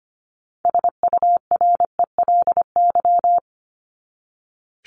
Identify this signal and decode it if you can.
Morse code, heard as a keyed tone, about 25 wpm, 700 Hz. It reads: SURELY
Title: LY Suffix - Top 100 Words with Base Words 25wpm